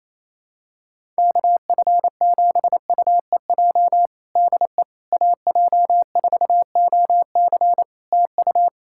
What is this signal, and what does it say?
Morse code, heard as a keyed tone, about 28 wpm, 700 Hz. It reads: KF7UEJ DE AJ4OC TU
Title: Practice QSO - Exchange 4 - Ending - 28wpm